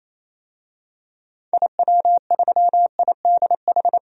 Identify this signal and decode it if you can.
Morse code, heard as a keyed tone, about 28 wpm, 700 Hz. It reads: IW3IDH
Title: DXCC-100 Call Signs - 6 Characters - 28wpm